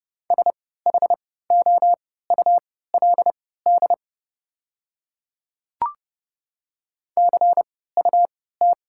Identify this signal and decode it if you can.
Morse code, heard as a keyed tone, about 30 wpm, spaced as for 17 wpm, 700 Hz. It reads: SHOULD CUT
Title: 3x Character Spacing - Top 500 Words - 30wpm